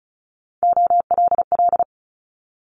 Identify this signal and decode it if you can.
Morse code, heard as a keyed tone, about 36 wpm, 700 Hz. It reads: OLL
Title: Multi-letter Phonemes Quick Fire 35wpm